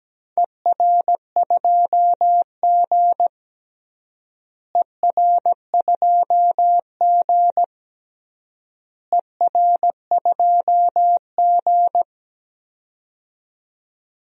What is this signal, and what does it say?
Morse code, heard as a keyed tone, about 17 wpm, 700 Hz. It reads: ER2G ER2G ER2G
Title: DXCC-100 Call Signs - 4 Characters - Repeated 3x - 17wpm